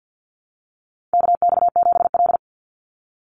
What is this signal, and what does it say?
Morse code, heard as a keyed tone, about 49 wpm, 700 Hz. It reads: KX7L
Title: NA SOTA Chaser Callsigns 50wpm